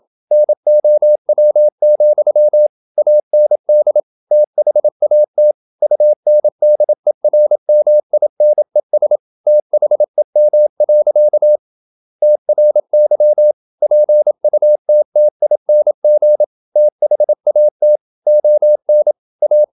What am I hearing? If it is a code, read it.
Morse code, heard as a keyed tone, about 27 wpm, 600 Hz. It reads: NOW, AND THAT UNDERMINES THEM. TRY PUTTING THAT ON A
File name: Never 0027wpm